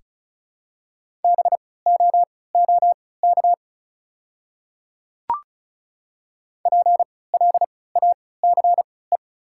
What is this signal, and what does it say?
Morse code, heard as a keyed tone, about 36 wpm, spaced as for 20 wpm, 690 Hz. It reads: BOOK PLACE
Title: 3x Character Spacing - Top 500 Words - 35wpm